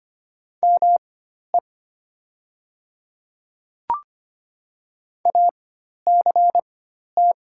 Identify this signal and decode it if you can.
Morse code, heard as a keyed tone, about 25 wpm, spaced as for 12 wpm, 700 Hz. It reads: ME ACT
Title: 4x Character Spacing - Top 200 Words - 25wpm